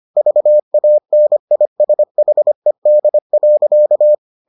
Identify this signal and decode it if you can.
Morse code, heard as a keyed tone, about 25 wpm, 600 Hz. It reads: VANISHED.